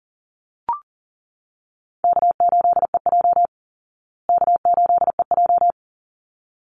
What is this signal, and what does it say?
Morse code, heard as a keyed tone, about 40 wpm, 700 Hz. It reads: K8EJ K8EJ
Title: Call Signs Repeated 3x - 40wpm